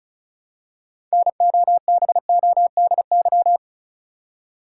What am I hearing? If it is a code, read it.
Morse code, heard as a keyed tone, about 36 wpm, 700 Hz. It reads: NOBODY
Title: Six Letter Words - 35wpm